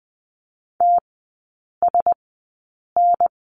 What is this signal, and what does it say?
Morse code, heard as a keyed tone, about 20 wpm, spaced as for 8 wpm, 700 Hz. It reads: TSN
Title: 035 Lesson 7e - Letter s - Three Characters 20wpm